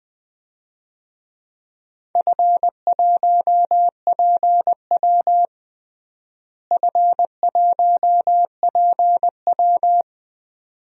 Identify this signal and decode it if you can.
Morse code, heard as a keyed tone, about 21 wpm, 700 Hz. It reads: F1PW F1PW
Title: DXCC-100 Call Signs - 4 Characters - Repeated 2x - 20wpm